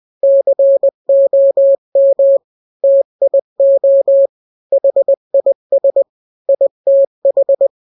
Morse code, heard as a keyed tone, about 20 wpm, 550 Hz. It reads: COM TIO HIS ITH
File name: n-grams / 3-grams 20 18 WPM